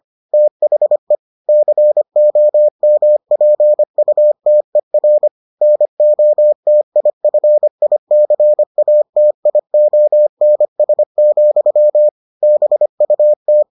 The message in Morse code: THE COMPUTER NOTIFICATIONS, BUT

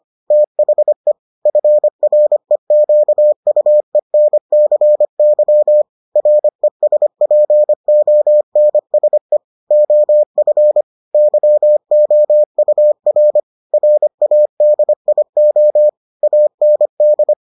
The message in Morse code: THE FREQUENCY RESPONSE OF YOUR RADIO AND

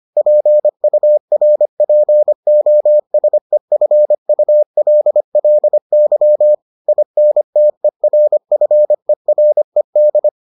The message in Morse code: PURPOSEFULLY INTERFERED

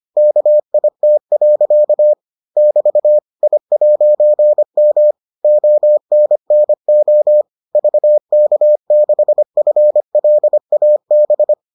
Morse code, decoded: KIT. <BT> I'M ONNO VK6FLAB

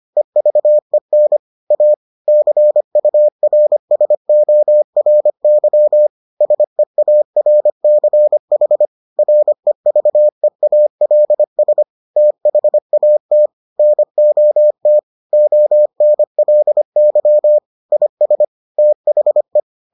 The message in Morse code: EVEN A CURSORY SEARCH REVEALS THAT NOT ONLY IS THE

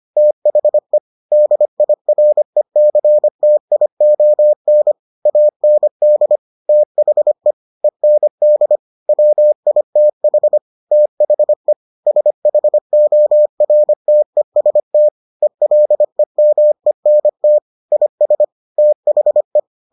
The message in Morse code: THE DIRECTION AND THE END WITH THE SHORTEST ELEMENT IS THE

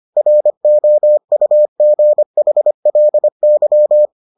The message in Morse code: ROUGHLY

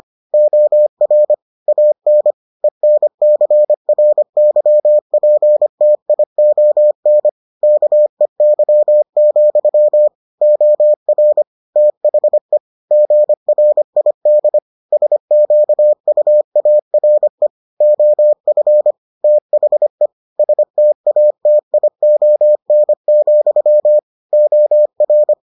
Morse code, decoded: OR AN ENCRYPTION KEY, OR THE GRID SQUARE OF THE STATION, OR

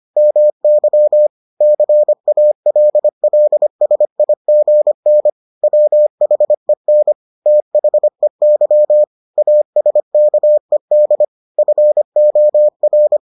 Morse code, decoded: MY CALLSIGN WHEN THEY ASKED FOR

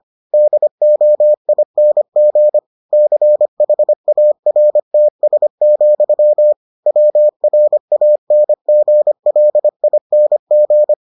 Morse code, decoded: DOING CHARTS, WRANGLING

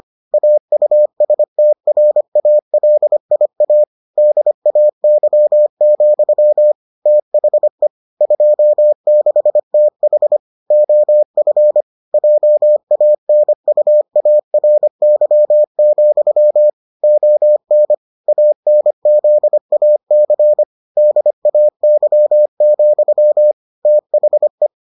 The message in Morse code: AUSTRALIA DAY, THE 26TH OF JANUARY, ON ANZAC DAY, THE

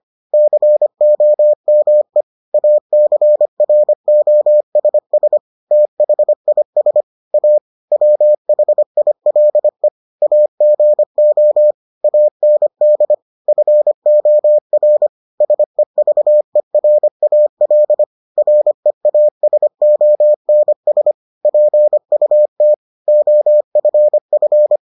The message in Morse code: COME ACROSS THIS A WHILE AGO AND FOR SEVERAL REASONS PUT OFF

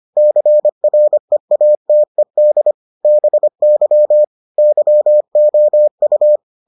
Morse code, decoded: CREATED BY YOU

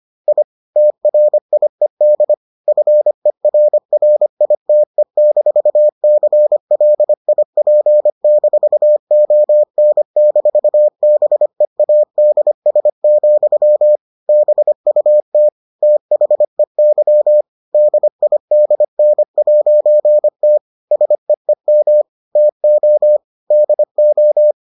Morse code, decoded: I TRIED FERRITE-CLIP-ON-BEADS, BUT THEY DIDN'T SEEM TO DO

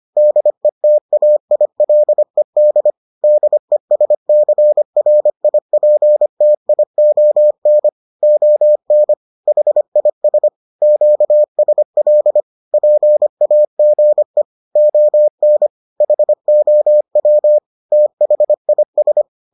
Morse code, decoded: DETAILED DESCRIPTION ON HIS QSL PAGE ON HOW THIS

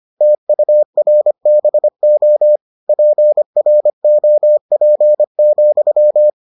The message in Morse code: TURBO PROP,

